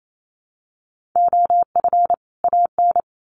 Morse code, heard as a keyed tone, about 28 wpm, 700 Hz. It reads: OF AN
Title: Sets of 2 Words 28wpm